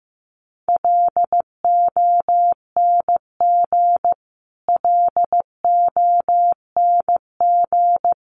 Morse code, heard as a keyed tone, about 15 wpm, 700 Hz. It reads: LONG LONG
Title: Top 500 Words Repeated 15wpm